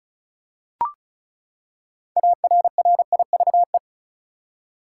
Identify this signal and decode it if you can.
Morse code, heard as a keyed tone, about 36 wpm, 700 Hz. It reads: ARRIVE
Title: Six Letter Words - 35wpm